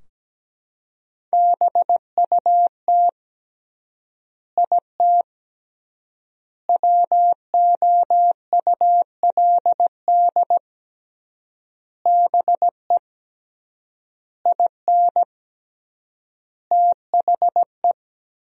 Morse code, heard as a keyed tone, about 17 wpm, 700 Hz. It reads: BUT IT WOULD BE IN THE
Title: 3x Word-Spacing - Sentences from Top 300 Words - 17wpm